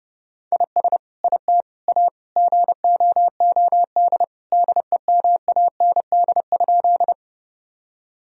Morse code, heard as a keyed tone, about 30 wpm, 700 Hz. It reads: IS IT A GOOD DEMAND?
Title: Sentences from Top 1000 Words - Review Part 4 - 30wpm